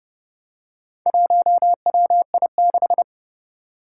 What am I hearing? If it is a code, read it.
Morse code, heard as a keyed tone, about 30 wpm, 700 Hz. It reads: 1WI6
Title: Four Challenging Letter-Number Combinations 30wpm